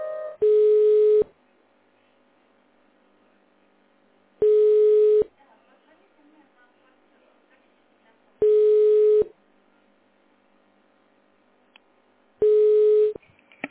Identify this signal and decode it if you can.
Ringback tone (425 Hz, 1 s on, 3 s off), with speech.